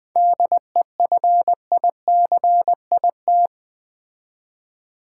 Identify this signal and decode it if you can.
Morse code, heard as a keyed tone, about 20 wpm, 700 Hz. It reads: DEFICIT
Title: 7 Letter Words - First 3 Letters Given - 20wpm